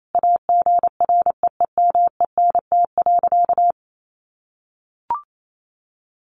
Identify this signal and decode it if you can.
Morse code, heard as a keyed tone, about 28 wpm, 700 Hz. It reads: AGREEMENT.
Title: Sentences from Top 1500 Words 28wpm